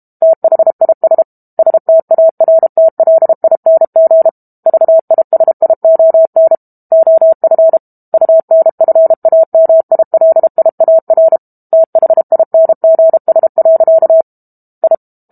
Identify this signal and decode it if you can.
Morse code, heard as a keyed tone, about 32 wpm, 650 Hz. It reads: THIS STARTLING VISION OF UNFAMILIAR THINGS. I
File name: Book Two / 06-Chapter 6 - 32 WPM